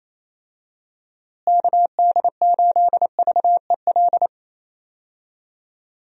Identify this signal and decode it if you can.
Morse code, heard as a keyed tone, about 28 wpm, 700 Hz. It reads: KD8VEL
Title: USA Call Signs - 6 Characters - Repeated 3x - 28wpm